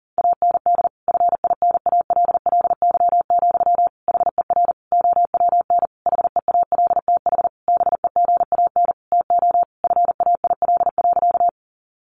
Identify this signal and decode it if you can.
Morse code, heard as a keyed tone, about 40 wpm, 700 Hz. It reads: AND FINALLY, HER OWN HEALTH BEGAN TO FAIL.
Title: Sentences from Top 1100 Words 40wpm